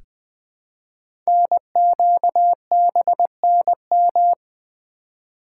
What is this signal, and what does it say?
Morse code, heard as a keyed tone, about 20 wpm, 700 Hz. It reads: NQBNM
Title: Code Groups - 5 Letters - 20wpm